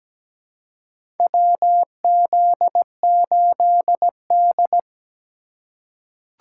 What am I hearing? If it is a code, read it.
Morse code, heard as a keyed tone, about 17 wpm, 700 Hz. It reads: WZ8D
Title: USA Call Signs - 4 Characters - 2x Character Spacing - 17wpm